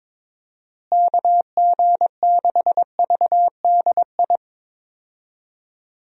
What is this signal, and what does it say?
Morse code, heard as a keyed tone, about 22 wpm, 700 Hz. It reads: KG6VDI
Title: USA Call Signs - 6 Characters - Repeated 3x - 22wpm